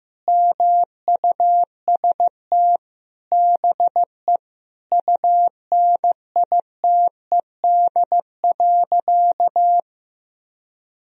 Morse code, heard as a keyed tone, about 16 wpm, 700 Hz. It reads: MUST BE UNITED.
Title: Sentences from Top 1000 Words - Review Part 4 - 15wpm